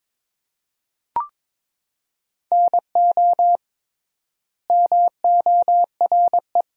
Morse code, heard as a keyed tone, about 22 wpm, 700 Hz. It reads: NO MORE
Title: 3x Word-Spacing - Sentences from Top 400 Words - 22wpm